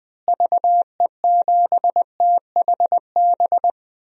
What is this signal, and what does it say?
Morse code, heard as a keyed tone, about 20 wpm, 700 Hz. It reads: VE7THB
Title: DXCC-100 Call Signs - 6 Characters - Repeated 2x - 20wpm